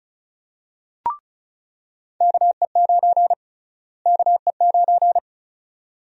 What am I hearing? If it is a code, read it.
Morse code, heard as a keyed tone, about 36 wpm, 700 Hz. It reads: KE9 KE9
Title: USA Call Sign Prefixes - Repeated 2x - 35wpm